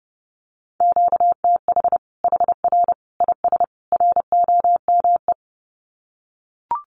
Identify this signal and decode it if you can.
Morse code, heard as a keyed tone, about 30 wpm, 700 Hz. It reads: QTH HR IS ROME
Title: Common QSO Phrases 30wpm